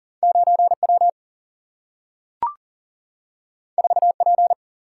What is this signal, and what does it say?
Morse code, heard as a keyed tone, about 40 wpm, 700 Hz. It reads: 9 W 4 P
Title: Single-Letter Single-Number 40wpm